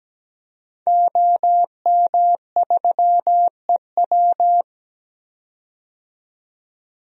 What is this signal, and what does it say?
Morse code, heard as a keyed tone, about 17 wpm, 700 Hz. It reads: OM3EW
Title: DXCC-100 Call Signs - 5 Characters - 17wpm